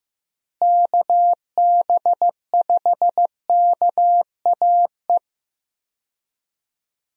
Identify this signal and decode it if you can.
Morse code, heard as a keyed tone, about 15 wpm, 700 Hz. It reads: KB5KAE
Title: USA Call Signs - 6 Characters - Repeated 2x - 15wpm